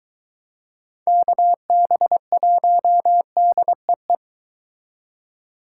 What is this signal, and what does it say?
Morse code, heard as a keyed tone, about 22 wpm, 700 Hz. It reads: KB1DEE